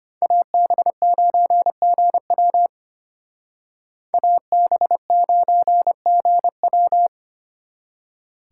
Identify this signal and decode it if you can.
Morse code, heard as a keyed tone, about 32 wpm, 700 Hz. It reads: AB9GW AB9GW